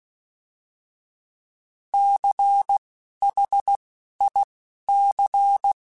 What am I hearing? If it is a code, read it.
Morse code, heard as a keyed tone, about 16 wpm, spaced as for 11 wpm, 800 Hz. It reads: CHIC
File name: Alphabetical / Alphabetical mp3s / Cs @10